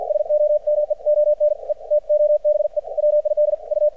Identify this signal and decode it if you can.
Morse code, heard as a keyed tone, about 53 wpm, spaced as for 46 wpm, 600 Hz. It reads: SOMEONE TO BE WID U
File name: archived / CRSnet